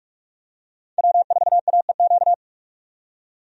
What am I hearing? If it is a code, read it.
Morse code, heard as a keyed tone, about 43 wpm, 700 Hz. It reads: W4AEQ
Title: USA Call Signs - 5 Characters - 45wpm